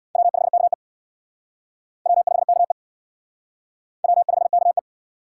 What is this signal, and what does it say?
Morse code, heard as a keyed tone, about 58 wpm, 700 Hz. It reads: W5KE W5KE W5KE
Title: Speed Racing - USA 1x2 Call Signs 50wpm